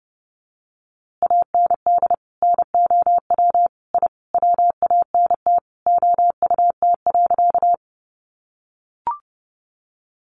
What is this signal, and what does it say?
Morse code, heard as a keyed tone, about 30 wpm, 700 Hz. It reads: AND NOW I WANT OUT.
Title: A Sound Of Thunder 30wpm